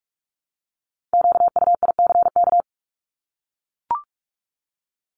Morse code, heard as a keyed tone, about 44 wpm, 700 Hz. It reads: QUICK
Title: Top 500 Words 45wpm